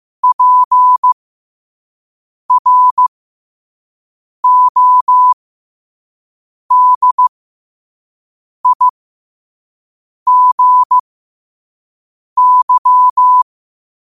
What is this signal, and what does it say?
Morse code, heard as a keyed tone, about 15 wpm, spaced as for 5 wpm, 1000 Hz. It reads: PRODIGY
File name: qotdfiles / 5wpm